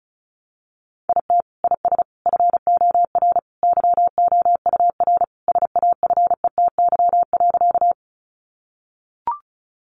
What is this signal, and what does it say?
Morse code, heard as a keyed tone, about 34 wpm, 700 Hz. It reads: IT IS FOR YOUR SAFETY.
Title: Sentences from Top 700 Words 35wpm